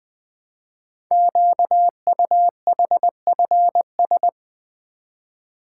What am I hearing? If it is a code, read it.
Morse code, heard as a keyed tone, about 20 wpm, 700 Hz. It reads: QUHFS
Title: Code Groups - 5 Letters - 20wpm